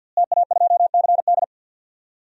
Morse code, heard as a keyed tone, about 48 wpm, 700 Hz. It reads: TA2CD
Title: DXCC-100 Call Signs - 5 Characters - Repeated 2x - 50wpm